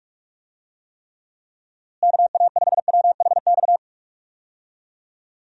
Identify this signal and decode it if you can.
Morse code, heard as a keyed tone, about 46 wpm, 700 Hz. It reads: KA5WHX
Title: USA Call Signs - 6 Characters - Repeated 3x - 45wpm